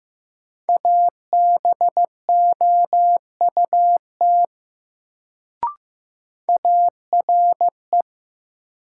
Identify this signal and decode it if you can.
Morse code, heard as a keyed tone, about 15 wpm, 700 Hz. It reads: ABOUT ARE
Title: Sets of 2 Words - New Word Drill - 15wpm